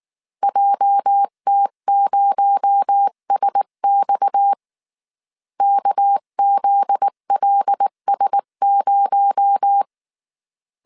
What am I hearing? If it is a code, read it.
Morse code, heard as a keyed tone, about 19 wpm, 800 Hz. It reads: JT0SX KZLS0